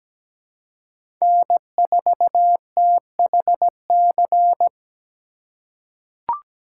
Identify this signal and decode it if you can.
Morse code, heard as a keyed tone, about 17 wpm, 700 Hz. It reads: N4THC
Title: USA Call Signs - 5 Characters - 2x Character Spacing - 17wpm